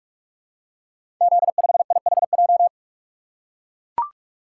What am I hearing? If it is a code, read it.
Morse code, heard as a keyed tone, about 44 wpm, 690 Hz. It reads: Z5IHJ
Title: Code Groups - 5 Letter:Numbers - 45wpm